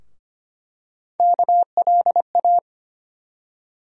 Morse code, heard as a keyed tone, about 25 wpm, 700 Hz. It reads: KLA
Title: USA Call Sign Postfixes - 3 Letters 25wpm